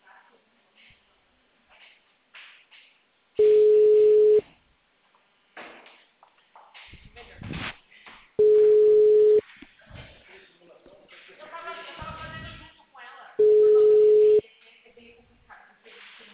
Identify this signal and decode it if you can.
Ringback tone (Europe, 425 Hz), with speech.